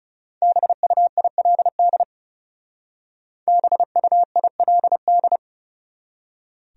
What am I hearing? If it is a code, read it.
Morse code, heard as a keyed tone, about 40 wpm, 700 Hz. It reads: BUILD BUILD